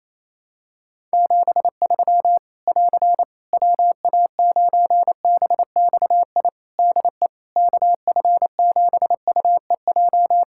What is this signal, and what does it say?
Morse code, heard as a keyed tone, about 28 wpm, 700 Hz. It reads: 73 <AR> WA9BXI DE KF7UEJ
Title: K1USN SST QSO Practice 28wpm